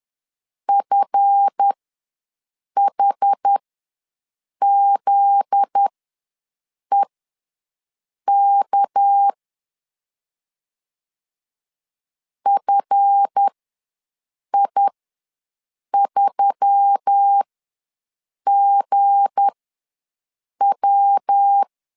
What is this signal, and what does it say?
Morse code, heard as a keyed tone, about 10 wpm, spaced as for 6 wpm, 800 Hz. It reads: FHZEK FI3GW
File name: LEZIONI / 7 GF9Z / 7 GF9Z ws 10wpm 3919